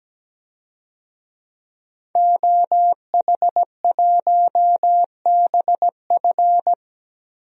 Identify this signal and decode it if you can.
Morse code, heard as a keyed tone, about 17 wpm, 700 Hz. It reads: OH1BF
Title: DXCC-100 Call Signs - 5 Characters - 17wpm